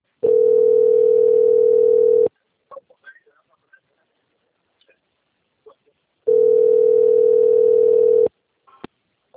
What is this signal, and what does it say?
Ringback tone (North America), 440 + 480 Hz.
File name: out- - - - - .1773